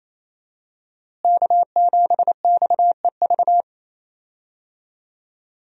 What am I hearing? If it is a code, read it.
Morse code, heard as a keyed tone, about 28 wpm, 700 Hz. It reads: K7XEV
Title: USA Call Signs - 5 Characters - Repeated 2x - 28wpm